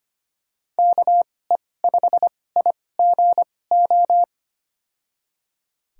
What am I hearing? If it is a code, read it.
Morse code, heard as a keyed tone, about 25 wpm, spaced as for 18 wpm, 700 Hz. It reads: KE5IGO
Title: USA Call Signs - 6 Characters - 2x Character Spacing - 25wpm